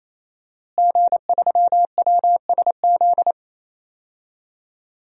Morse code, heard as a keyed tone, about 27 wpm, 700 Hz. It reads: G3WSZ
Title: DXCC-100 Call Signs - 5 Characters - Repeated 3x - 28wpm